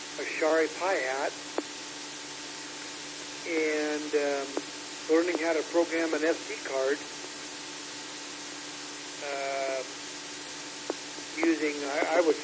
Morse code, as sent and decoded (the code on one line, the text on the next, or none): none